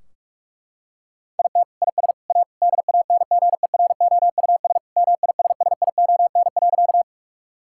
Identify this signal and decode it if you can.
Morse code, heard as a keyed tone, about 42 wpm, 700 Hz. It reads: IT IS A DANGEROUS MISSION.
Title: Sentences from Top 1600 Words 45wpm